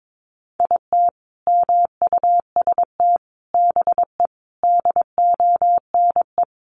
Morse code, heard as a keyed tone, about 22 wpm, 700 Hz. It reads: IT MUST BE DONE
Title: Sentences from Top 1100 Words 22wpm